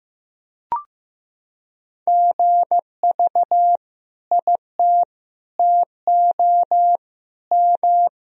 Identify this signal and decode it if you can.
Morse code, heard as a keyed tone, about 16 wpm, 700 Hz. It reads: GV IT TO M
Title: Sentences from Top 200 Words and Abbreviations 15wpm